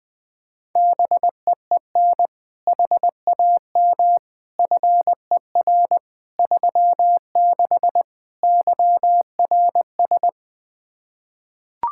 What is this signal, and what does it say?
Morse code, heard as a keyed tone, about 20 wpm, 700 Hz. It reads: BEEN HAM FER 36 YRS
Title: Practice QSO - Exchange 3 - More Info - 20wpm